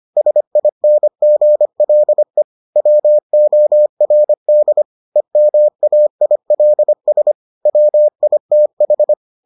Morse code, heard as a keyed tone, about 25 wpm, 600 Hz. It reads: SINGLE WORD EMAILS WITH